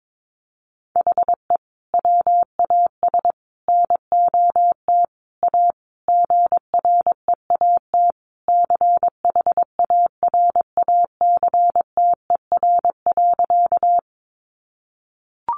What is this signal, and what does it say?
Morse code, heard as a keyed tone, about 22 wpm, 700 Hz. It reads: HE WAS NOT A GREAT CHARACTER.
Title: Sentences from Top 1200 Words 22wpm